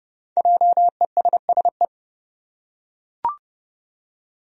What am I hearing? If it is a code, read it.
Morse code, heard as a keyed tone, about 30 wpm, 700 Hz. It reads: JESSE
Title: USA Common Names - 30wpm